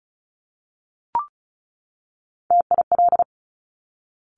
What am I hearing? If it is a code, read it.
Morse code, heard as a keyed tone, about 36 wpm, 700 Hz. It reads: TIL
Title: Top Three Letter Combinations in Words 35wpm